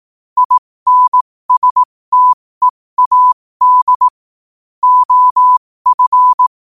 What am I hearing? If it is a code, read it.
Morse code, heard as a keyed tone, about 18 wpm, spaced as for 15 wpm, 1000 Hz. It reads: INSTEAD OF
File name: qotdfiles / 15wpm